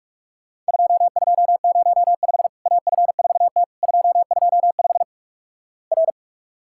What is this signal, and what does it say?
Morse code, heard as a keyed tone, about 44 wpm, 700 Hz. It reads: 2205 AF4T 225 R
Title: Worked All Europe DX Contest - QTC Exchange - 45wpm